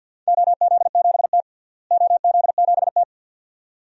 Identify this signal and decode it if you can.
Morse code, heard as a keyed tone, about 52 wpm, 700 Hz. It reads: OZ7T OZ7T
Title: DXCC-100 Call Signs - 4 Characters - Repeated 2x - 50wpm